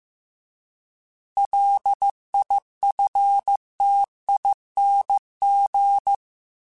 Morse code, heard as a keyed tone, about 15 wpm, 800 Hz. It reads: LIFTING